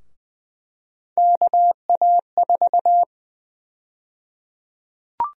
Morse code, heard as a keyed tone, about 20 wpm, 700 Hz. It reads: KA4
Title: USA Call Sign Prefixes - Repeated 2x - 20wpm